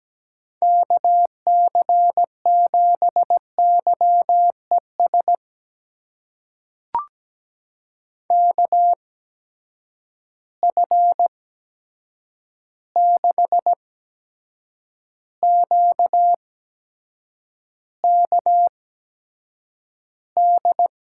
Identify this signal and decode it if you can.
Morse code, heard as a keyed tone, about 17 wpm, spaced as for 7 wpm, 700 Hz. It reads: KC7YES KF6QKD
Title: USA Call Signs - 6 Characters - 8x Character Spacing - 17wpm